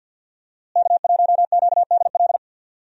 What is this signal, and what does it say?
Morse code, heard as a keyed tone, about 52 wpm, 700 Hz. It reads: K1QDL
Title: USA Call Signs - 5 Characters - Repeated 3x - 50wpm